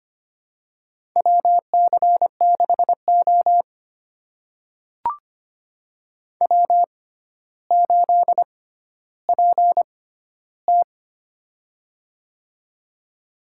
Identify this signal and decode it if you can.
Morse code, heard as a keyed tone, about 25 wpm, spaced as for 13 wpm, 700 Hz. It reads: WC6O W8PT
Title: USA Call Signs - 4 Characters - 6x Character Spacing - 25wpm